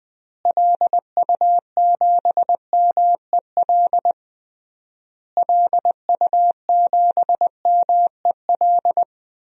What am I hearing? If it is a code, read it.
Morse code, heard as a keyed tone, about 20 wpm, 700 Hz. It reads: LU7MEL LU7MEL
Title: DXCC-100 Call Signs - 6 Characters - Repeated 3x - 20wpm